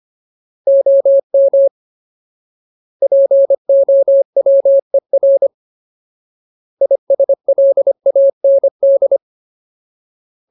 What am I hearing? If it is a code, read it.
Morse code, heard as a keyed tone, about 25 wpm, 550 Hz. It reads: OM POWER ISLAND
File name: Ultimate Word List / 25WPM Ultimate